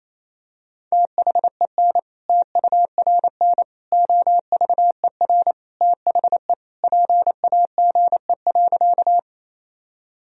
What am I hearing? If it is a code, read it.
Morse code, heard as a keyed tone, about 28 wpm, 700 Hz. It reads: THEN TURN OVER THE PAGE.
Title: Sentences from Top 1000 Words - Review Part 1 - 28wpm